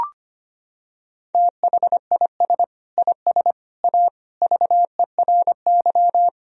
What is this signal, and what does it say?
Morse code, heard as a keyed tone, about 25 wpm, 700 Hz. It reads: THIS IS A VERY
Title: Sentences from Top 1000 Words - Review Part 3 - 25wpm